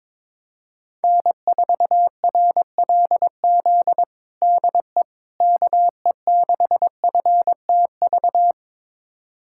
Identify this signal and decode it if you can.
Morse code, heard as a keyed tone, about 22 wpm, 700 Hz. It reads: N4RLZ DE KE6FTV
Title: USA Callsign DE Callsign 22wpm